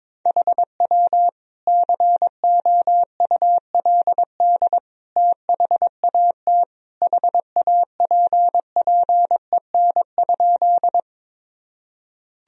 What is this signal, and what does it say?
Morse code, heard as a keyed tone, about 22 wpm, 700 Hz. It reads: HW COULD THAT HAPPEN?
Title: Sentences from Top 400 Words and Abbreviations 22wpm